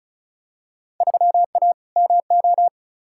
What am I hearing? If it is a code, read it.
Morse code, heard as a keyed tone, about 36 wpm, 700 Hz. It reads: 3A MO
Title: Field Day Exchanges - 35wpm